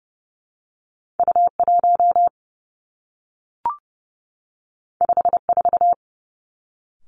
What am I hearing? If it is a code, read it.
Morse code, heard as a keyed tone, about 30 wpm, 700 Hz. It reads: U 1 5 4
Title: Two Challenging Letter-Number Combinations 30wpm